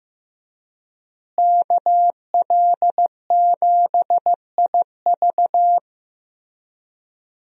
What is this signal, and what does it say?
Morse code, heard as a keyed tone, about 15 wpm, 700 Hz. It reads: KL7IV
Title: USA Call Signs - 5 Characters - Repeated 3x - 15wpm